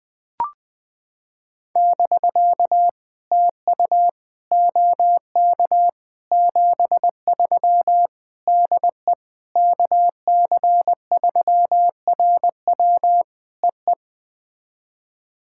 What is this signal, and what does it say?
Morse code, heard as a keyed tone, about 20 wpm, 700 Hz. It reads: <BK> TU OK 73 DE KC3RW EE
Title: POTA QSO Practice - 20wpm